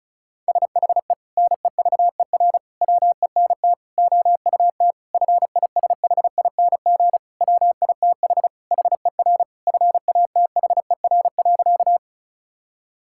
Morse code, heard as a keyed tone, about 34 wpm, 700 Hz. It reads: SHE NEVER WENT OUT FISHING WITH HER FATHER.
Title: Sentences from Top 1000 Words - Review Part 5 - 35wpm